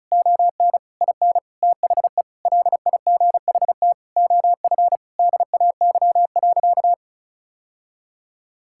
Morse code, heard as a keyed tone, about 36 wpm, 700 Hz. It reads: ON IN THE LIGHT OF DAY.
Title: Sentences from Top 1000 Words - Review Part 5 - 35wpm